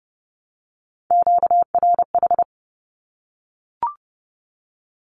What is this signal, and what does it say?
Morse code, heard as a keyed tone, about 30 wpm, 700 Hz. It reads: QRH
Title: Q-Codes Repeated 3x 30wpm